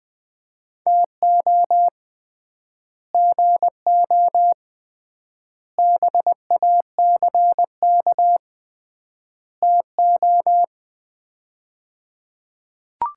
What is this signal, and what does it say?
Morse code, heard as a keyed tone, about 21 wpm, 700 Hz. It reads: TO GO BACK TO
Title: 3x Word-Spacing - Sets of 4 Words - 20wpm